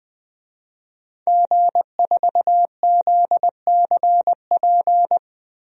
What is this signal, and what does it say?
Morse code, heard as a keyed tone, about 20 wpm, 700 Hz. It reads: G4ZCP
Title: DXCC-100 Call Signs - 5 Characters - Repeated 2x - 20wpm